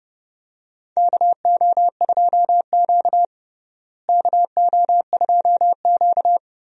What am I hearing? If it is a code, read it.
Morse code, heard as a keyed tone, about 30 wpm, 700 Hz. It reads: KO2Q KO2Q
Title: USA Call Signs - 4 Characters - Repeated 2x - 30wpm